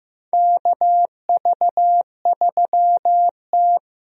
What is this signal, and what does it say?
Morse code, heard as a keyed tone, about 15 wpm, 700 Hz. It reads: KV3T
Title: USA Call Signs - 4 Characters - Repeated 2x - 15wpm